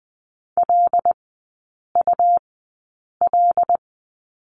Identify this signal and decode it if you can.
Morse code, heard as a keyed tone, about 20 wpm, spaced as for 8 wpm, 700 Hz. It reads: LUL
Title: 114 Lesson 19e - Letter w - Three Characters 20wpm